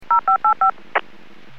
DTMF, heard as telephone touch tones, tones of about 110 ms. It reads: 0585
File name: WPS / 00223 13-07 WPS 451.25 WPS